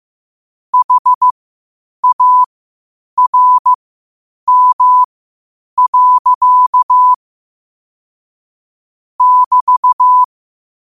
Morse code, heard as a keyed tone, about 15 wpm, spaced as for 8 wpm, 1000 Hz. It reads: HARM. <BT>
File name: 8wpm